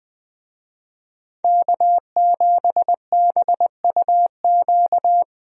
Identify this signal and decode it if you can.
Morse code, heard as a keyed tone, about 20 wpm, 700 Hz. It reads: K7BUQ
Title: USA Call Signs - 5 Characters - Repeated 2x - 20wpm